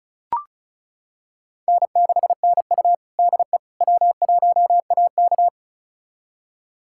Morse code, heard as a keyed tone, about 36 wpm, 700 Hz. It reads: N6NU DE W1AK
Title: USA Callsign DE Callsign - 4 Characters - 35wpm